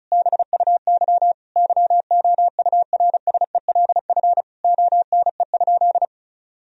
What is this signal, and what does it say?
Morse code, heard as a keyed tone, about 35 wpm, 700 Hz. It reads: BUY YOURSELF ONE?
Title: Sentences from Top 1000 Words - Review Part 4 - 35wpm